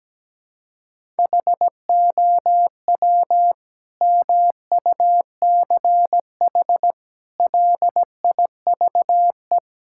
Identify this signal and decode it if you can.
Morse code, heard as a keyed tone, about 17 wpm, 700 Hz. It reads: HOW MUCH LIVE
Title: Sentences from Top 1000 Words - Review Part 5 - 17wpm